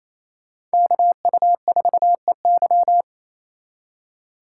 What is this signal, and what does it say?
Morse code, heard as a keyed tone, about 35 wpm, 700 Hz. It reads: KU4EY